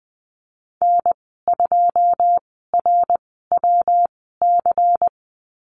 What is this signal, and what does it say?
Morse code, heard as a keyed tone, about 20 wpm, spaced as for 15 wpm, 700 Hz. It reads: N2RWC
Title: 154 Lesson 25g - Letter p - Call Signs 20wpm